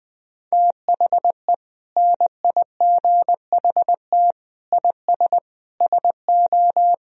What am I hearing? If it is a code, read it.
Morse code, heard as a keyed tone, about 21 wpm, 700 Hz. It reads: THE NIGHT IS SO
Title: Sentences from Top 500 Words and Abbreviations 20wpm